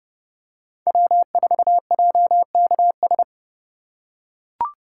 Morse code, heard as a keyed tone, about 30 wpm, 700 Hz. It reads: W4JKS
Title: USA Call Signs - 5 Characters - 2x Character Spacing - 30wpm